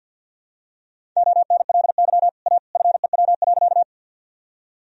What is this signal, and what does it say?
Morse code, heard as a keyed tone, about 51 wpm, 700 Hz. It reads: ONLY A FEW.
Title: Sentences from Top 1000 Words - Review Part 1 - 50wpm